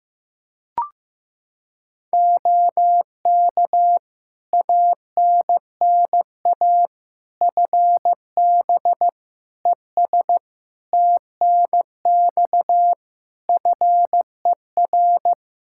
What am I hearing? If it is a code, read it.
Morse code, heard as a keyed tone, about 15 wpm, 700 Hz. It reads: OK ANNA FB ES TNX FER
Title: Practice QSO - Exchange 2 - Info - 15wpm